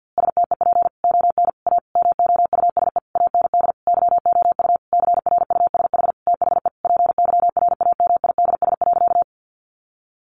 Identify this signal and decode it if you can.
Morse code, heard as a keyed tone, about 50 wpm, 700 Hz. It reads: STEP ON A MOUSE AND YOU CRUSH THE PYRAMIDS.
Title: A Sound Of Thunder 50wpm